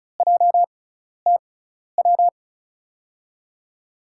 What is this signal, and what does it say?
Morse code, heard as a keyed tone, about 36 wpm, spaced as for 12 wpm, 700 Hz. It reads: JTW
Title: USA Call Signs - 5 Characters - 6x Character Spacing - 35wpm